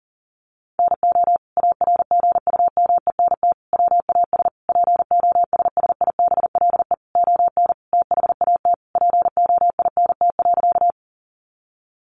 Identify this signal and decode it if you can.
Morse code, heard as a keyed tone, about 40 wpm, 700 Hz. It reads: NO ARGUMENT WAS POSSIBLE ON THAT POINT.
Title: Sentences from Top 1400 Words 40wpm